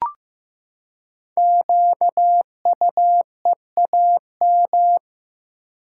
Morse code, heard as a keyed tone, about 15 wpm, 700 Hz. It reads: QUEAM
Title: Code Groups - 5 Letters - 15wpm